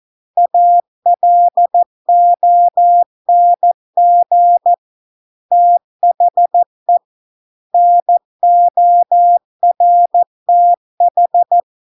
Morse code, heard as a keyed tone, about 14 wpm, 700 Hz. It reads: ALONG THE NORTH